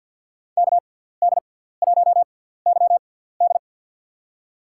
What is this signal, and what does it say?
Morse code, heard as a keyed tone, about 52 wpm, spaced as for 17 wpm, 700 Hz. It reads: KD1YD
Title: USA Call Signs - 5 Characters - 6x Character Spacing - 50wpm